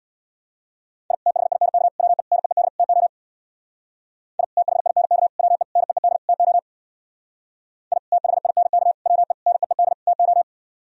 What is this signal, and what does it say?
Morse code, heard as a keyed tone, about 65 wpm, 700 Hz. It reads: I THINK WE NEED TO I THINK WE NEED TO I THINK WE NEED TO